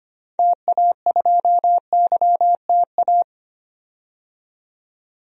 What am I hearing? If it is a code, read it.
Morse code, heard as a keyed tone, about 25 wpm, 700 Hz. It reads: TA2YTA
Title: DXCC-100 Call Signs - 6 Characters - 25wpm